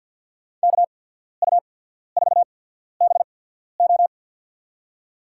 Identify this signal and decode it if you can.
Morse code, heard as a keyed tone, about 52 wpm, spaced as for 14 wpm, 700 Hz. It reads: KU4BO
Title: USA Call Signs - 5 Characters - 8x Character Spacing - 50wpm